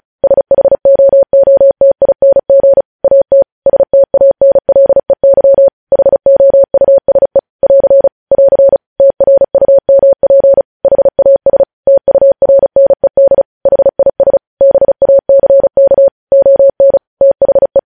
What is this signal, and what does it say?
Morse code, heard as a keyed tone, about 36 wpm, 550 Hz. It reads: SHOOTING AT STANLEY HOUSE <AR> <AR> TRUMP HAS TURNED HIS BACK ON THE
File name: podcast / News headlines-1x-35x35